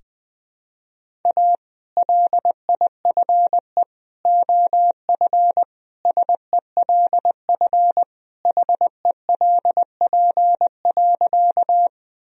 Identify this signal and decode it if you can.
Morse code, heard as a keyed tone, about 21 wpm, 700 Hz. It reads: A LIFE OF SELF HELP.
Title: Sentences from Top 1000 Words - Review Part 1 - 20wpm